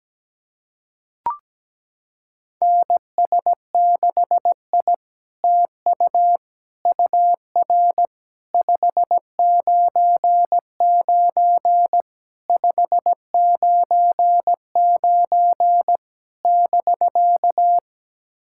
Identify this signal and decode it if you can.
Morse code, heard as a keyed tone, about 17 wpm, 690 Hz. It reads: NS6I TU UR 599 599 <BK>
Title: SOTA QSO Practice 17wpm